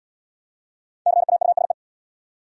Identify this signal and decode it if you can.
Morse code, heard as a keyed tone, about 69 wpm, 700 Hz. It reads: CAUSE